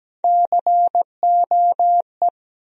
Morse code, heard as a keyed tone, about 17 wpm, 700 Hz. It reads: COE
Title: USA Call Sign Postfixes - 3 Letters 17wpm